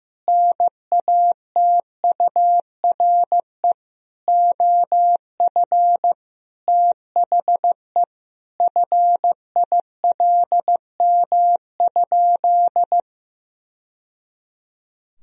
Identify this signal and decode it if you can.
Morse code, heard as a keyed tone, about 15 wpm, 700 Hz. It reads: NATURE OF THE FILM?
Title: Sentences from Top 1000 Words - Review Part 3 - 15wpm